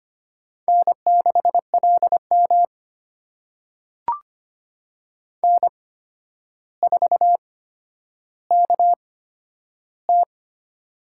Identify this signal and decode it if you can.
Morse code, heard as a keyed tone, about 25 wpm, spaced as for 11 wpm, 700 Hz. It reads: N6LM N4KT
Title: USA Call Signs - 4 Characters - 8x Character Spacing - 25wpm